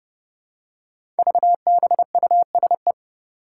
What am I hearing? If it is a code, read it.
Morse code, heard as a keyed tone, about 30 wpm, 700 Hz. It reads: VBUSE
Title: Five Challenging Letter-Number Combinations 30wpm